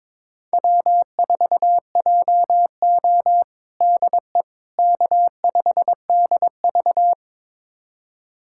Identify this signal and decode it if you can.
Morse code, heard as a keyed tone, about 22 wpm, 700 Hz. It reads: W4JO DE K5DV
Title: USA Callsign DE Callsign - 4 Characters - 22wpm